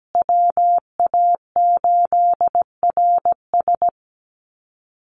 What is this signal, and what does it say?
Morse code, heard as a keyed tone, about 17 wpm, 700 Hz. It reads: WA8RS
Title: USA Call Signs Repeated 3x 17wpm